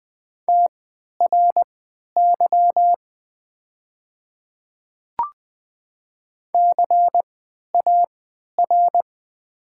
Morse code, heard as a keyed tone, about 20 wpm, spaced as for 11 wpm, 700 Hz. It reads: TRY CAR
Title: 3x Character Spacing - Top 500 Words - 20wpm